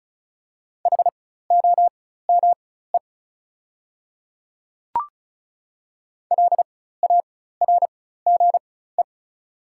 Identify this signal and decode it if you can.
Morse code, heard as a keyed tone, about 36 wpm, spaced as for 16 wpm, 690 Hz. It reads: HOME LARGE
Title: 4x Character Spacing - Top 500 Words - 35wpm